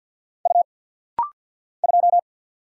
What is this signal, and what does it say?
Morse code, heard as a keyed tone, about 50 wpm, 690 Hz. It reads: U 2
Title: Single Letter-Number Alt - Rapid-Fire 50wpm